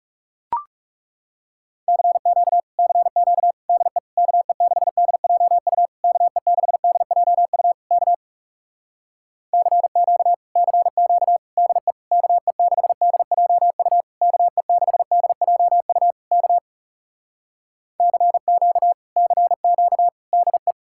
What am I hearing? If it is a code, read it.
Morse code, heard as a keyed tone, about 40 wpm, 700 Hz. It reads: CQ CQ DE KE6DJU KE6DJU K CQ CQ DE KE6DJU KE6DJU K CQ CQ DE